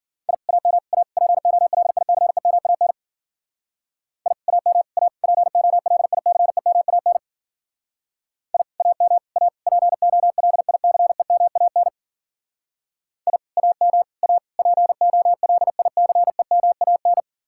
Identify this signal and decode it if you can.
Morse code, heard as a keyed tone, about 52 wpm, 700 Hz. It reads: I AM A POLICEMAN I AM A POLICEMAN I AM A POLICEMAN I AM A POLICEMAN